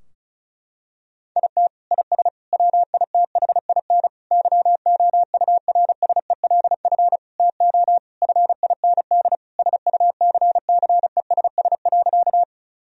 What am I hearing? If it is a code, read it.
Morse code, heard as a keyed tone, about 36 wpm, 700 Hz. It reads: IT IS WITHIN YOURSELF TO FIND SUCCESS.
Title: Sentences from Top 1000 Words - Review Part 4 - 35wpm